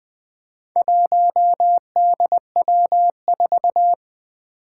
Morse code, heard as a keyed tone, about 21 wpm, 700 Hz. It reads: 1DW4
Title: Four Challenging Letter-Number Combinations 20wpm